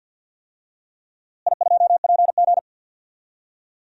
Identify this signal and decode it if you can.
Morse code, heard as a keyed tone, about 49 wpm, 700 Hz. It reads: I2PG
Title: DXCC-100 Call Signs - 4 Characters - Repeated 3x - 50wpm